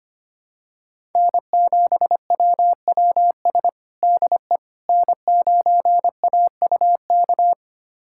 Morse code, heard as a keyed tone, about 25 wpm, 700 Hz. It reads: N7WWS DE N9AUK
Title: USA Callsign DE Callsign - 5 Characters - 25wpm